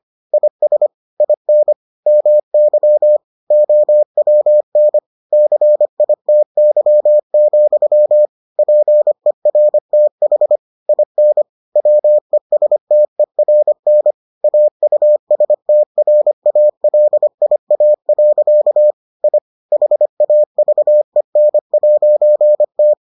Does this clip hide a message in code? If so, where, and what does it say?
Yes: Morse code from 0.3 to 22.9 s, heard as a keyed tone, about 25 wpm, 600 Hz. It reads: IS IN MY OWN CITY, PERTH IN WESTERN AUSTRALIA. I HAVEN'T